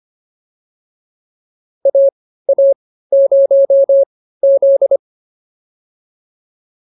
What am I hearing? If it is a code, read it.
Morse code, heard as a keyed tone, about 25 wpm, spaced as for 15 wpm, 550 Hz. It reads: AA0Z
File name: POTA201 15